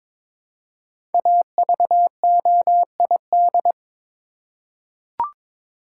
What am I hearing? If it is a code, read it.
Morse code, heard as a keyed tone, about 23 wpm, 700 Hz. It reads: AVOID
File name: 5 Letter Words - First 3 Letters Given - 22wpm